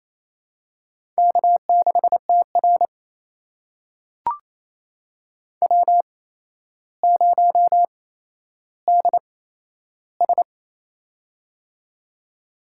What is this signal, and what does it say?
Morse code, heard as a keyed tone, about 28 wpm, spaced as for 12 wpm, 700 Hz. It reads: K6TR W0DS
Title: USA Call Signs - 4 Characters - 8x Character Spacing - 28wpm